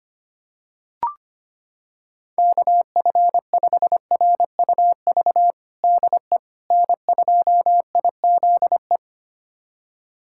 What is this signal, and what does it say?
Morse code, heard as a keyed tone, about 25 wpm, 700 Hz. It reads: KF5RUV DE N2IZE
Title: USA Callsign DE Callsign 25wpm